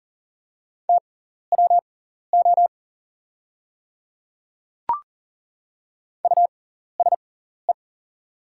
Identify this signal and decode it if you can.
Morse code, heard as a keyed tone, about 40 wpm, spaced as for 14 wpm, 690 Hz. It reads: TWO USE
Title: 6x Character Spacing - Top 500 Words - 40wpm